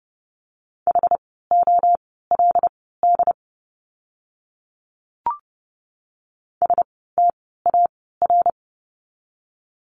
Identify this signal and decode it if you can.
Morse code, heard as a keyed tone, about 30 wpm, spaced as for 17 wpm, 700 Hz. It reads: HOLD STAR
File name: Lesson 17f - Letter u - Words 30wpm